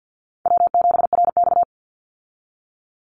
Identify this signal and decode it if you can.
Morse code, heard as a keyed tone, about 50 wpm, 700 Hz. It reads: W7RX
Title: Call Signs Repeated 2x - 50wpm